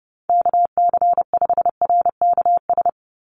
Morse code, heard as a keyed tone, about 30 wpm, 700 Hz. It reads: KC5RKS
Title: USA Call Signs 30wpm